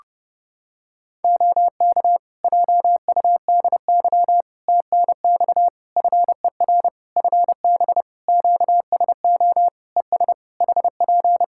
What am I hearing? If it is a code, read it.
Morse code, heard as a keyed tone, about 30 wpm, 700 Hz. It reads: OK JUDY TNX FER FB QSO ES HP
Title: Practice QSO - Exchange 4 - Ending - 30wpm